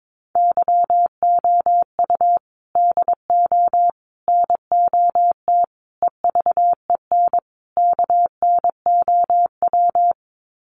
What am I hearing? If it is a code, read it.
Morse code, heard as a keyed tone, about 22 wpm, 700 Hz. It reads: YOU DO NOT EVEN KNOW
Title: Sets of 5 Words Encore2 22wpm